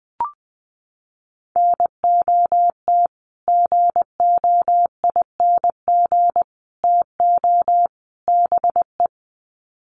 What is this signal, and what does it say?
Morse code, heard as a keyed tone, about 20 wpm, 700 Hz. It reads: NOT GOING TO BE
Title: Sets of 4 Words 20wpm